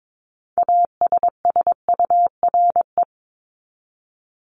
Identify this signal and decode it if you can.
Morse code, heard as a keyed tone, about 22 wpm, 700 Hz. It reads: ASSURE
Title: Top 1501-1750 Words 22wpm